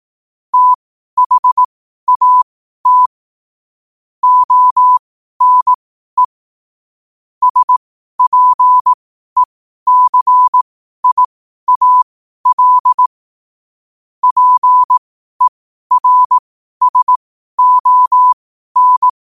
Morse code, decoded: THAT ONE SPECIAL PERSON

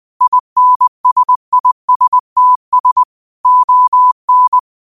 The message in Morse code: INSISTS ON

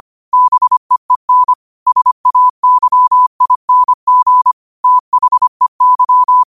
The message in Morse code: BEEN SAYING THEY

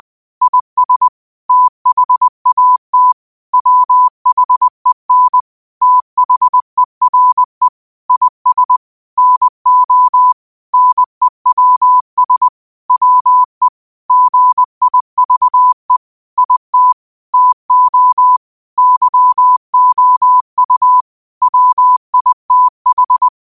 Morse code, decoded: IS THAT WHEN THERE IS NO NEWS WE GIVE IT TO YOU WITH

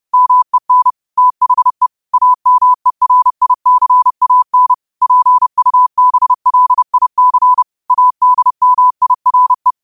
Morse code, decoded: MEN THE AMERICAN PUBLIC ADMIRE

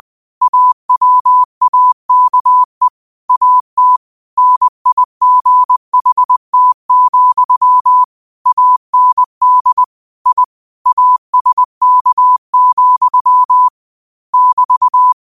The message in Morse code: AWAKE AT NIGHT, AND I ASK, <BT>

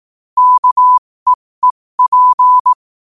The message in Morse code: KEEP

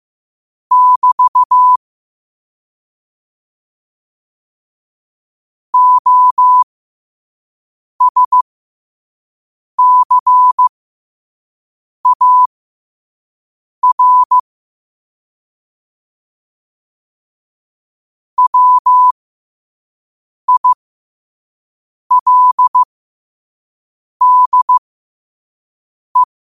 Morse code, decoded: <BT> OSCAR WILDE

